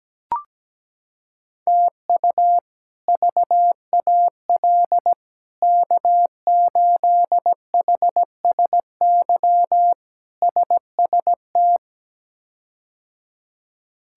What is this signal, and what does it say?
Morse code, heard as a keyed tone, about 17 wpm, 700 Hz. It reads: TU VAL K8HSY SST
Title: K1USN SST QSO Practice - 17wpm